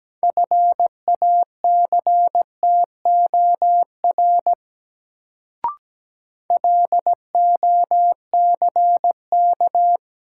Morse code, heard as a keyed tone, about 17 wpm, 700 Hz. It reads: FACTOR LOCK
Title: Sentences from Top 1300 Words - New Word Drill - 17wpm